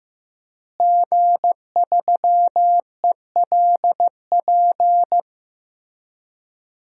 Morse code, heard as a keyed tone, about 15 wpm, 700 Hz. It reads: G3ELP
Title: DXCC-100 Call Signs - 5 Characters - Repeated 3x - 15wpm